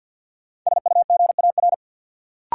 Morse code, heard as a keyed tone, about 47 wpm, 700 Hz. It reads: SUGAR
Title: Sentences from Top 1200 Words - New Word Drill - 50wpm